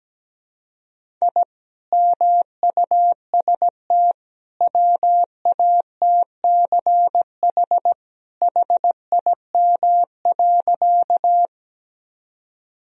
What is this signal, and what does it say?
Morse code, heard as a keyed tone, about 17 wpm, 700 Hz. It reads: I MUST WATCH HIM.
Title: Sentences from Top 1000 Words - Review Part 1 - 17wpm